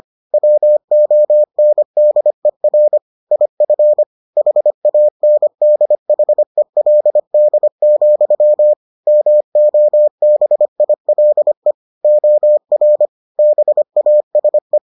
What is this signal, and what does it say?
Morse code, heard as a keyed tone, about 25 wpm, 600 Hz. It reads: WONDER IF HANDHELD, MOBILE OR BASE